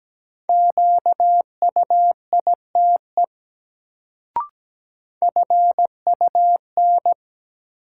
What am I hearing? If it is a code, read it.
Morse code, heard as a keyed tone, about 17 wpm, 700 Hz. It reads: QUITE FUN
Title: Sentences from Top 700 Words - New Word Drill - 17wpm